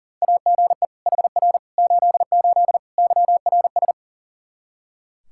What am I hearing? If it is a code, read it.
Morse code, heard as a keyed tone, about 37 wpm, 700 Hz. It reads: AGE HR 88 YRS
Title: Practice QSO - Exchange 3 - More Info - 40wpm